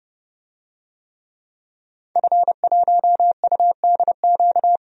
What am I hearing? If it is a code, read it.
Morse code, heard as a keyed tone, about 32 wpm, 700 Hz. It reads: F1UDQ
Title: DXCC-100 Call Signs - 5 Characters - 30wpm